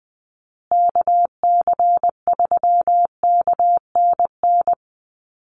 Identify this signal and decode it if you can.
Morse code, heard as a keyed tone, about 20 wpm, 700 Hz. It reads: KC3KNN
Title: USA Call Signs Repeated 3x 20wpm